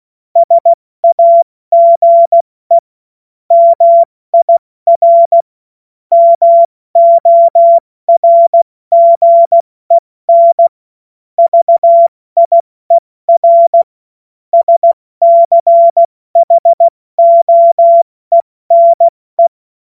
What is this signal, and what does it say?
Morse code, heard as a keyed tone, about 16 wpm, 680 Hz. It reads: SAGE MIR MORGEN VIER SCHOENE
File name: Morsekurs / Witze / Witz 10. August 2021, 70 F80